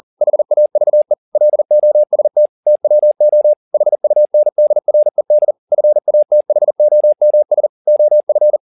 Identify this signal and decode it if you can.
Morse code, heard as a keyed tone, about 40 wpm, 600 Hz. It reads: HAVE LOST TWO HUNDRED FATHOMS OF